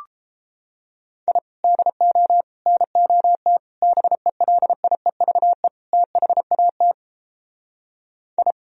Morse code, heard as a keyed tone, about 31 wpm, 700 Hz. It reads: I DO NOT BELIEVE THAT I